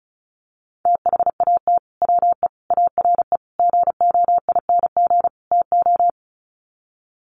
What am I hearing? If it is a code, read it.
Morse code, heard as a keyed tone, about 34 wpm, 700 Hz. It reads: THAT WE ARE GOING TO
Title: Sets of 5 Words Encore 35wpm